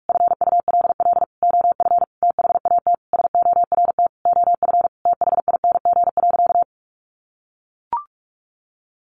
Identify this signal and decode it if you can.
Morse code, heard as a keyed tone, about 44 wpm, 700 Hz. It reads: FULL OF THAT SORT OF THING.
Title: Sentences from Top 1300 Words 45wpm